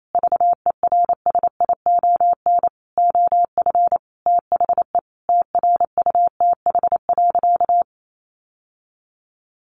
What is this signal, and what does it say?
Morse code, heard as a keyed tone, about 28 wpm, 700 Hz. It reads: VERSION OF THE TRUTH.
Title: Sentences from Top 1400 Words 28wpm